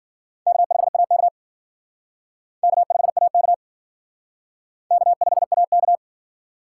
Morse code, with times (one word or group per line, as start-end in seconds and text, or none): 0.5-1.3 K5AK
2.6-3.5 K5AK
4.9-6.0 K5AK